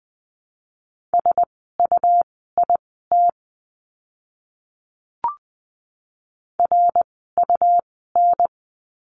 Morse code, heard as a keyed tone, about 20 wpm, spaced as for 14 wpm, 700 Hz. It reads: SUIT RUN
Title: 101 Lesson 17f - Letter u - Words 20wpm